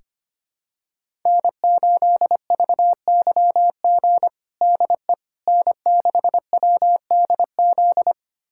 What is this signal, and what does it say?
Morse code, heard as a keyed tone, about 25 wpm, 700 Hz. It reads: N8VYG DE N6WDZ
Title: USA Callsign DE Callsign - 5 Characters - 25wpm